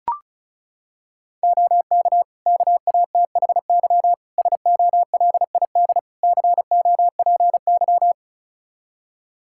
Morse code, heard as a keyed tone, about 34 wpm, 700 Hz. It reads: OK KATHY SOLID COPY
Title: Practice QSO - Exchange 3 - More Info - 35wpm